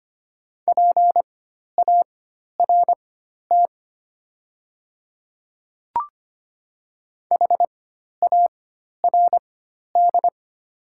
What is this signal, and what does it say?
Morse code, heard as a keyed tone, about 25 wpm, spaced as for 12 wpm, 690 Hz. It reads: PART HARD
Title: 4x Character Spacing - Top 300 Words - 25wpm